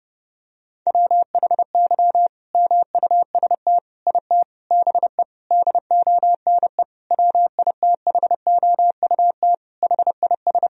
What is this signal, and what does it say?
Morse code, heard as a keyed tone, about 32 wpm, 700 Hz. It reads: WHY MUST IT BE DONE WITHOUT HIS
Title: Sentences from Top 1000 Words - Review Part 3 - 30wpm